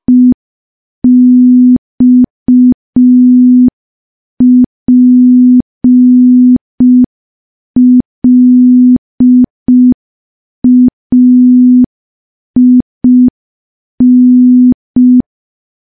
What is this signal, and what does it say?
Morse code, heard as a keyed tone, about 5 wpm, 250 Hz. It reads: EXPLAIN
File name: morse / morse-training-05WPM